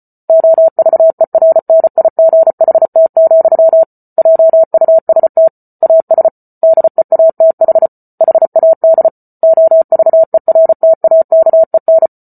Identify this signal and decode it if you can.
Morse code, heard as a keyed tone, about 34 wpm, 650 Hz. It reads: OVERNIGHT, JUST AS DEATH HAD OVERTAKEN